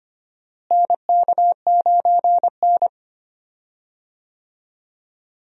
Morse code, heard as a keyed tone, about 25 wpm, 700 Hz. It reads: NK9N
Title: USA Call Signs - 4 Characters - Repeated 3x - 25wpm